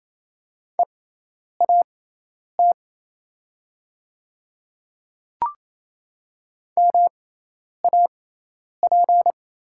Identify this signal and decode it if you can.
Morse code, heard as a keyed tone, about 28 wpm, spaced as for 10 wpm, 700 Hz. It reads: EAT MAP
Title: 6x Character Spacing - Top 500 Words - 28wpm